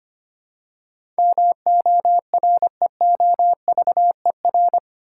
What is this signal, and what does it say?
Morse code, heard as a keyed tone, about 25 wpm, 700 Hz. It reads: MOREOVER
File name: Eight Letter Words - 25wpm